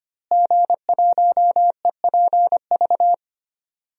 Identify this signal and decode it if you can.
Morse code, heard as a keyed tone, about 25 wpm, 700 Hz. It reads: G1EPV
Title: DXCC-100 Call Signs - 5 Characters - Repeated 2x - 25wpm